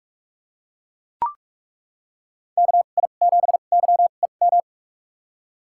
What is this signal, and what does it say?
Morse code, heard as a keyed tone, about 41 wpm, spaced as for 31 wpm, 700 Hz. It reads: KI7YEM
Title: USA Call Signs - 6 Characters - 2x Character Spacing - 45wpm